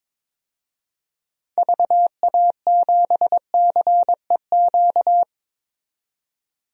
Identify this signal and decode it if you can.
Morse code, heard as a keyed tone, about 22 wpm, 700 Hz. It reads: VA7CEQ
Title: DXCC-100 Call Signs - 6 Characters - 22wpm